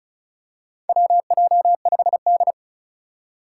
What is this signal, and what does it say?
Morse code, heard as a keyed tone, about 36 wpm, 700 Hz. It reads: WJ5D
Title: USA Call Signs - 4 Characters - 3x Character Spacing - 35wpm